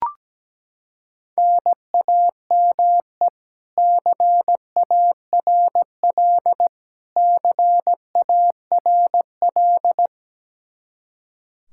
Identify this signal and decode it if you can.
Morse code, heard as a keyed tone, about 17 wpm, 700 Hz. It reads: NAME CARL CARL
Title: K1USN SST QSO Practice 17wpm